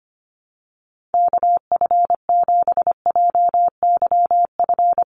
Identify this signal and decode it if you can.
Morse code, heard as a keyed tone, about 25 wpm, 700 Hz. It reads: KF7JYF
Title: USA Call Signs 25wpm